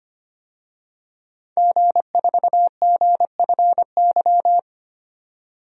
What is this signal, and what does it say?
Morse code, heard as a keyed tone, about 25 wpm, 700 Hz. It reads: G4GFY
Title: DXCC-100 Call Signs - 5 Characters - 25wpm